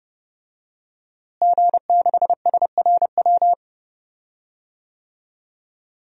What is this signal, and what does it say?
Morse code, heard as a keyed tone, about 30 wpm, 700 Hz. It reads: G6SRW
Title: DXCC-100 Call Signs - 5 Characters - Repeated 2x - 30wpm